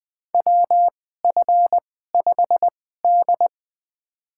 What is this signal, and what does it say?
Morse code, heard as a keyed tone, about 20 wpm, spaced as for 15 wpm, 700 Hz. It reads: WF5D
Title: USA Call Signs - 2x Character Spacing - 20wpm